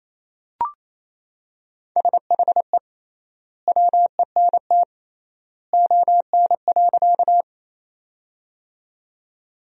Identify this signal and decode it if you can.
Morse code, heard as a keyed tone, about 28 wpm, 700 Hz. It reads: SHE WENT ON.
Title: 3x Word-Spacing - Sentences from Top 200 Words - 28wpm